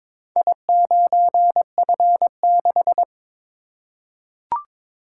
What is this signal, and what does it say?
Morse code, heard as a keyed tone, about 23 wpm, 700 Hz. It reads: I9F6
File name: practice / 22wpm / Four Letter-Number Combinations - 22wpm